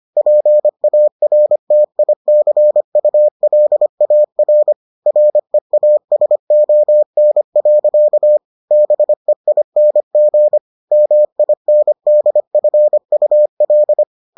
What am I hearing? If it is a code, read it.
Morse code, heard as a keyed tone, about 25 wpm, 600 Hz. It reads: PARTICULAR REASON. BEING MINDFUL